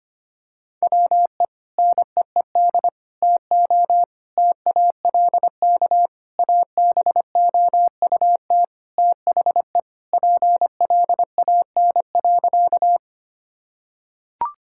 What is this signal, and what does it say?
Morse code, heard as a keyed tone, about 25 wpm, 700 Hz. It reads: WE NEED TO TALK ABOUT THE PLAN.
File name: practice / 25wpm / Sentences with 4-Word Phrases - 25wpm